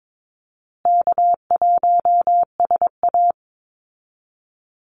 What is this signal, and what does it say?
Morse code, heard as a keyed tone, about 22 wpm, 700 Hz. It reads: K1SA
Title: Easy Call Signs Repeated 3x - 22wpm